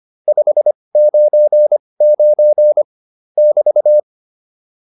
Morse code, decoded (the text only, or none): 599 <BT>